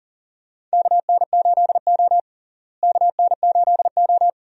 Morse code, heard as a keyed tone, about 37 wpm, 700 Hz. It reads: KN8O KN8O
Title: USA Call Signs - 4 Characters - Repeated 2x - 40wpm